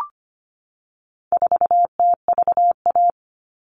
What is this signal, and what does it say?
Morse code, heard as a keyed tone, about 25 wpm, 700 Hz. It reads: 4TVA
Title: All Cut Numbers - 4-Digit 25wpm